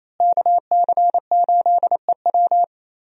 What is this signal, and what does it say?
Morse code, heard as a keyed tone, about 28 wpm, 700 Hz. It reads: KC8EW
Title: USA Call Signs - 5 Characters - Repeated 3x - 28wpm